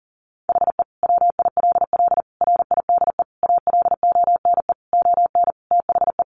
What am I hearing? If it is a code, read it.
Morse code, heard as a keyed tone, about 40 wpm, 700 Hz. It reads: HE WILL RIDE ALONE ON THE
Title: Sentences from Top 1400 Words 40wpm